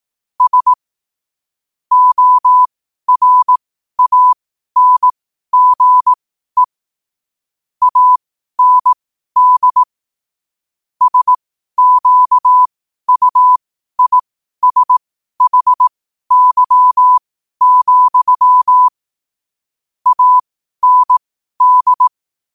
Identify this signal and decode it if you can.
Morse code, heard as a keyed tone, about 18 wpm, spaced as for 12 wpm, 1000 Hz. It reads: S ORANGE AND SQUISHY, AND
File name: qotdfiles / 12wpm